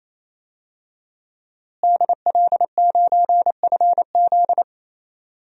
Morse code, heard as a keyed tone, about 28 wpm, 700 Hz. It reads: DL9FZ
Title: DXCC-100 Call Signs - 5 Characters - Repeated 3x - 28wpm